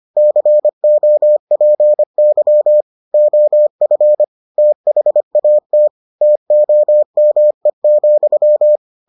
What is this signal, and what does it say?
Morse code, heard as a keyed tone, about 25 wpm, 600 Hz. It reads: COPY OF THAT TOME,